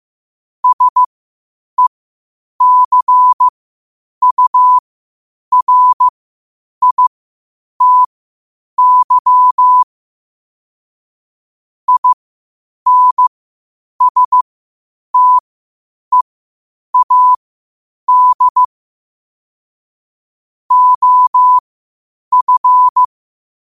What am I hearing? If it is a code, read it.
Morse code, heard as a keyed tone, about 15 wpm, spaced as for 8 wpm, 1000 Hz. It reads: SECURITY INSTEAD OF
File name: qotdfiles / 8wpm